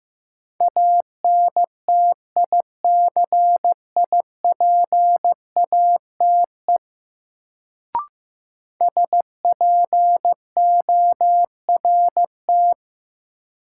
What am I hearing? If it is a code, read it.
Morse code, heard as a keyed tone, about 15 wpm, 700 Hz. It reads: ANTICIPATE SPORT
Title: Sentences from Top 1500 Words - New Word Drill Alt - 15wpm